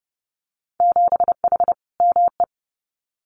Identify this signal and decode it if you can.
Morse code, heard as a keyed tone, about 30 wpm, 700 Hz. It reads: 7H ME
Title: Winter Field Day Exchanges Encore 30wpm